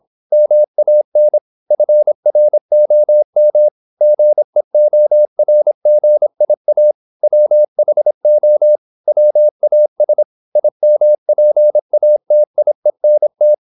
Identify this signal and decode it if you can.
Morse code, heard as a keyed tone, about 26 wpm, 600 Hz. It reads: MAN FROM GEORGIA WHO WAS IMPATIENT
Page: Never 26wpm | CW med Gnister